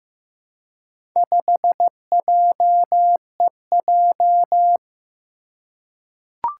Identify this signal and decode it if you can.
Morse code, heard as a keyed tone, about 15 wpm, 700 Hz. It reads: 5JEJ
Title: Four Challenging Letter-Number Combinations 15wpm